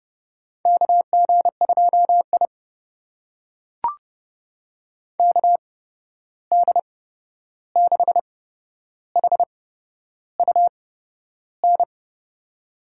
Morse code, heard as a keyed tone, about 31 wpm, spaced as for 11 wpm, 700 Hz. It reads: KG2I KD6HUN
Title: USA Call Signs - 8x Character Spacing - 30wpm